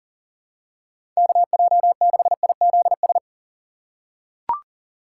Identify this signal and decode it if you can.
Morse code, heard as a keyed tone, about 37 wpm, 700 Hz. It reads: KJ6IZS
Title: USA Call Signs - 6 Characters - 4x Character Spacing - 40wpm